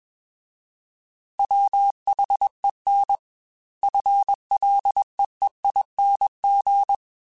Morse code, heard as a keyed tone, about 21 wpm, 800 Hz. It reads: WHEN FLEEING